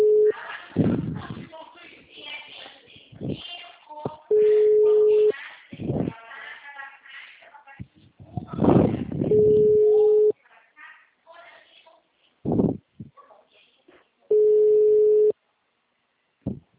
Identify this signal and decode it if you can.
Ringback tone (Europe, 425 Hz), with speech.